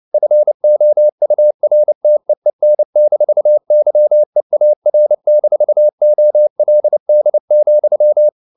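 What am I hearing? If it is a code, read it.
Morse code, heard as a keyed tone, about 29 wpm, 600 Hz. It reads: FOURTEEN-YEAR-OLD,
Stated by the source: Morse code at 29 wpm, sending Ken Follett's novel Never in English, from CW med Gnister.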